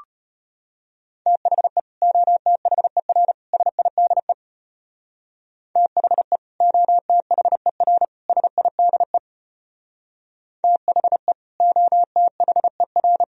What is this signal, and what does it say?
Morse code, heard as a keyed tone, about 33 wpm, 700 Hz. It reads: THE OTHER SIDE THE OTHER SIDE THE OTHER